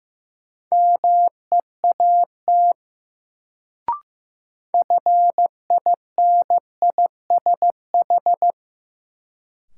Morse code, heard as a keyed tone, about 15 wpm, 700 Hz. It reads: MEAT FINISH
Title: Sentences from Top 1200 Words - New Word Drill Alt - 15wpm